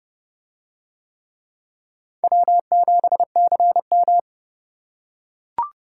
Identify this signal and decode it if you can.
Morse code, heard as a keyed tone, about 29 wpm, 700 Hz. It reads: W7CM